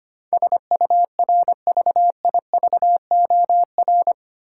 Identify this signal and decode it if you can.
Morse code, heard as a keyed tone, about 25 wpm, 700 Hz. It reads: SURVIVOR
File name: Eight Letter Words - 25wpm